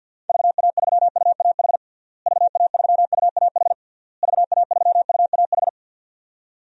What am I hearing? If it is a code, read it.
Morse code, heard as a keyed tone, about 52 wpm, 700 Hz. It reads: VA3UAH VA3UAH VA3UAH
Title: DXCC-100 Call Signs - 6 Characters - Repeated 3x - 50wpm